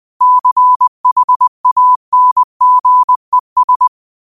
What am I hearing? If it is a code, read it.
Morse code, heard as a keyed tone, about 20 wpm, 1000 Hz. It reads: CHANGES